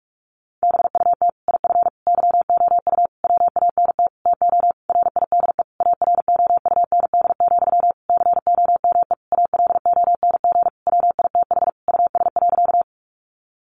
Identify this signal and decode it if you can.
Morse code, heard as a keyed tone, about 44 wpm, 700 Hz. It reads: BUT IF YOU WANT TO RIDE AROUND, COME ALONG WITH US.
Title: Sentences from Top 1200 Words 45wpm